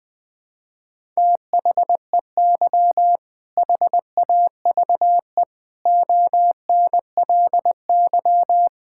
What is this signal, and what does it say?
Morse code, heard as a keyed tone, about 20 wpm, 700 Hz. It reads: THEY HAVE ONLY
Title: Sentences from Top 1000 Words - Review Part 5 - 20wpm